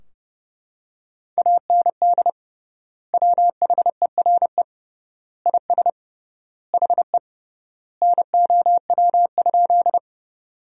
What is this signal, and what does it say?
Morse code, heard as a keyed tone, about 30 wpm, 700 Hz. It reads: AND WHERE IS HE NOW?
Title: 3x Word-Spacing - Sentences from Top 200 Words - 30wpm